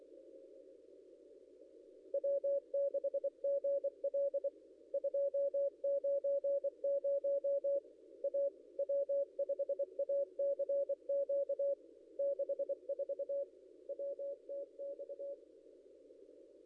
Morse code, heard as a keyed tone, about 24 wpm, 560 Hz. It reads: W6GL 290 A W5ACQ 64 WTX
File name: contests / ss-cw-k5tr-op-n5rz -0800